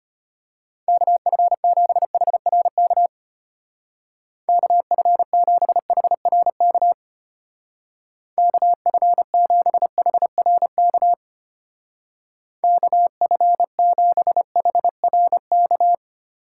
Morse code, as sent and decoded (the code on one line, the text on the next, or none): -.- ..-. --... .... .-. -.- / -.- ..-. --... .... .-. -.- / -.- ..-. --... .... .-. -.- / -.- ..-. --... .... .-. -.-
KF7HRK KF7HRK KF7HRK KF7HRK